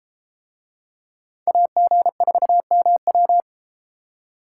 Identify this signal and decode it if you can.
Morse code, heard as a keyed tone, about 31 wpm, 700 Hz. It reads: AG4MW